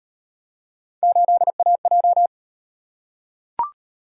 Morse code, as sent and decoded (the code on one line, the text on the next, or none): ---.. .- .---
8AJ